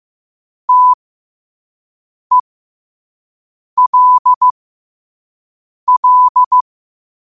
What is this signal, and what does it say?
Morse code, heard as a keyed tone, about 15 wpm, spaced as for 5 wpm, 1000 Hz. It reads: TELL
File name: qotdfiles / 5wpm